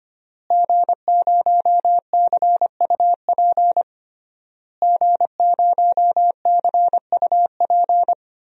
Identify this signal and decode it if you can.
Morse code, heard as a keyed tone, about 25 wpm, 700 Hz. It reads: G0CUP G0CUP
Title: DXCC-100 Call Signs - 5 Characters - Repeated 3x - 25wpm